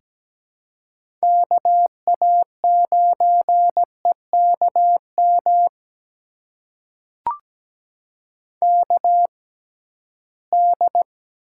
Morse code, heard as a keyed tone, about 17 wpm, spaced as for 13 wpm, 700 Hz. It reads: KA9EKM KD
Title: USA Call Signs - 6 Characters - 6x Character Spacing - 17wpm